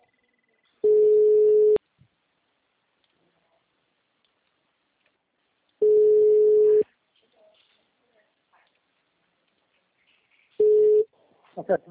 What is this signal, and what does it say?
Ringback tone (Europe, 425 Hz), with speech.
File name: out- -2017- - - .9472